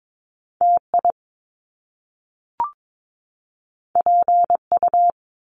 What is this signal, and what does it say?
Morse code, heard as a keyed tone, about 22 wpm, 700 Hz. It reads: TI PU
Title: Top Two Letter Combinations in Words 22wpm